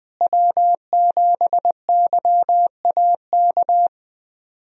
Morse code, heard as a keyed tone, about 21 wpm, 700 Hz. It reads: W7YAK
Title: USA Call Signs - 5 Characters - Repeated 3x - 20wpm